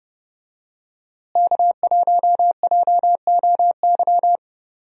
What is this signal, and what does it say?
Morse code, heard as a keyed tone, about 31 wpm, 700 Hz. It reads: K1JOY
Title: USA Call Signs - 5 Characters - Repeated 2x - 30wpm